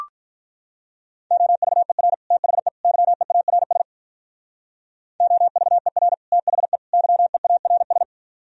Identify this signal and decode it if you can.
Morse code, heard as a keyed tone, about 44 wpm, 700 Hz. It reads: OVER THE YEARS OVER THE YEARS
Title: Speed-Racing - Sets of 3 Words 35wpm